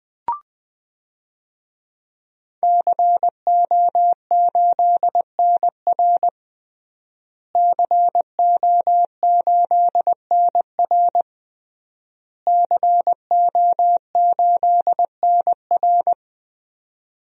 Morse code, heard as a keyed tone, about 21 wpm, 700 Hz. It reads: CO8NR CO8NR CO8NR
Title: DXCC-100 Call Signs - 5 Characters - Repeated 3x - 20wpm